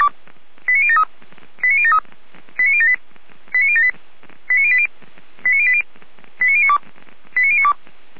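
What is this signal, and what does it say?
Five-tone selective call (ZVEI), heard as quick runs of five tones, tones of about 70 ms. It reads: 89073 89073 89078 89078 89080 89080 89092 89092
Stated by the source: Five-tone ZVEI selective call 89067 received on an SDR, from audio.f85360.feuerwehr-orlamuende.org, a fire brigade (Feuerwehr Orlamünde) server.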